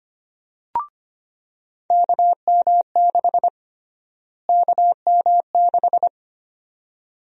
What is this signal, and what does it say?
Morse code, heard as a keyed tone, about 25 wpm, 700 Hz. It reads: KM6 KM6
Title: USA Call Sign Prefixes - Repeated 2x - 25wpm